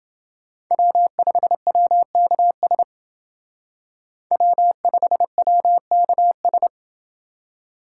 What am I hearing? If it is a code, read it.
Morse code, heard as a keyed tone, about 28 wpm, 700 Hz. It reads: W5WKS W5WKS